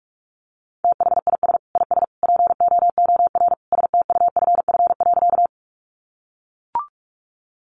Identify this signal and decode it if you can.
Morse code, heard as a keyed tone, about 44 wpm, 700 Hz. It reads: THIS IS POOR STUFF.
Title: Sentences from Top 1400 Words 45wpm